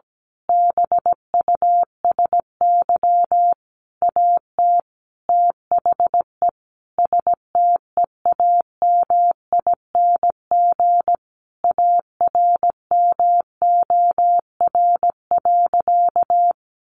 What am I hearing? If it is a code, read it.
Morse code, heard as a keyed tone, about 17 wpm, 700 Hz. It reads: BUSY AT THE STEAMING ARMOR.
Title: A Sound Of Thunder 17wpm